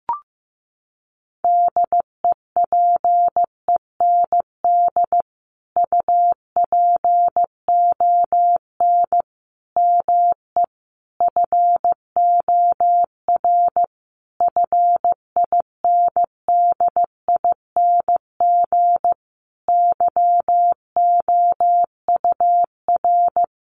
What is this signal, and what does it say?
Morse code, heard as a keyed tone, about 15 wpm, 700 Hz. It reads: DEPEND UPON ME FOR FINDING YOUR
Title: Sentences from Top 1300 Words 15wpm